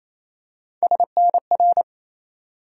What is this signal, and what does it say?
Morse code, heard as a keyed tone, about 28 wpm, 700 Hz. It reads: SNR
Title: USA Call Sign Postfixes - 3 Letters 28wpm